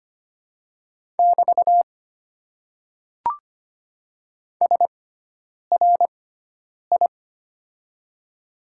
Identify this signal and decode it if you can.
Morse code, heard as a keyed tone, about 25 wpm, spaced as for 9 wpm, 700 Hz. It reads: <BT> SRI
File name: Top 90 QSO Elements - 6x Character Spacing - 25wpm